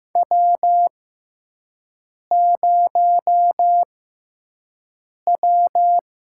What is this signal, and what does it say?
Morse code, heard as a keyed tone, about 15 wpm, spaced as for 5 wpm, 700 Hz. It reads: W0W
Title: USA Call Signs - 6x Character Spacing - 15wpm